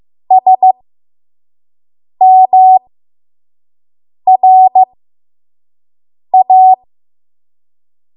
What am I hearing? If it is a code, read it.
Morse code, heard as a keyed tone, about 15 wpm, spaced as for 5 wpm, 750 Hz. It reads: SMRA
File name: Morse / 05WPM